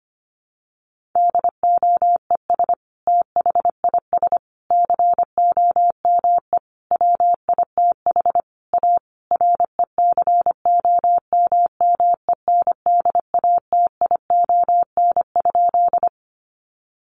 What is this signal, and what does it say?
Morse code, heard as a keyed tone, about 25 wpm, 700 Hz. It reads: DOES THIS COME WITH A RECOMMENDATION?
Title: Sentences from Top 1400 Words 25wpm